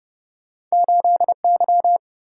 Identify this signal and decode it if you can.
Morse code, heard as a keyed tone, about 34 wpm, 700 Hz. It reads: 8Y